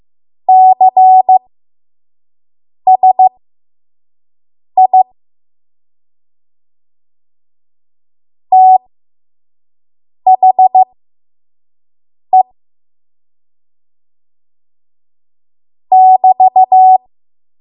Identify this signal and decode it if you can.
Morse code, heard as a keyed tone, about 15 wpm, spaced as for 5 wpm, 750 Hz. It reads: CSI THE <BT>
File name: Morse / 05WPM